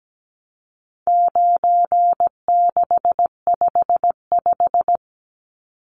Morse code, heard as a keyed tone, about 17 wpm, 700 Hz. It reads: 9655
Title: Numbers - 4-Digit 17wpm